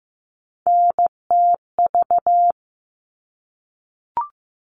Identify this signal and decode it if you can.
Morse code, heard as a keyed tone, about 15 wpm, 700 Hz. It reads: NTV
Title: All Cut Numbers - 3-Digit 15wpm